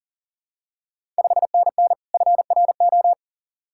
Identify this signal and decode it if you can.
Morse code, heard as a keyed tone, about 40 wpm, 690 Hz. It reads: 5NN FRO
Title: Ontario QSO Party Exchange 40wpm